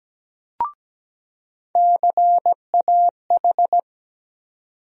Morse code, heard as a keyed tone, about 17 wpm, 700 Hz. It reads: CAH
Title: USA Call Sign Postfixes - 3 Letters 17wpm